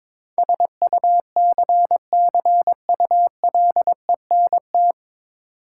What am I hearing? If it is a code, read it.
Morse code, heard as a keyed tone, about 22 wpm, 690 Hz. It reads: SUCCULENT
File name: Difficult Words with Hints - 22wpm